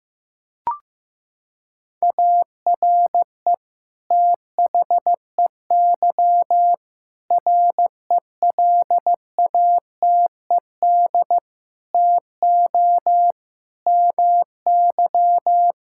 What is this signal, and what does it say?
Morse code, heard as a keyed tone, about 15 wpm, 700 Hz. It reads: ARE THEY RELATED TO MY
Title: Sentences from Top 1000 Words - Review Part 5 - 15wpm